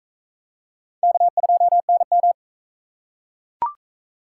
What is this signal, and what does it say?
Morse code, heard as a keyed tone, about 40 wpm, 700 Hz. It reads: K2NM